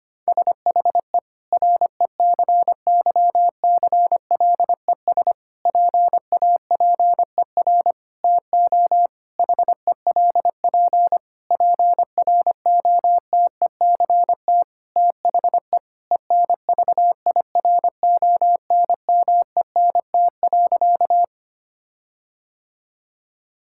Morse code, heard as a keyed tone, about 25 wpm, 700 Hz. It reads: SHE RECYCLES PAPER TO HELP PROTECT THE ENVIRONMENT.
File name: practice / 25wpm / Difficult Sentences with Hints - 25wpm